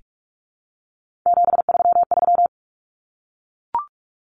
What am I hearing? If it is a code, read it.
Morse code, heard as a keyed tone, about 44 wpm, 700 Hz. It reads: 733
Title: Common Cut Numbers - 3-Digit 45wpm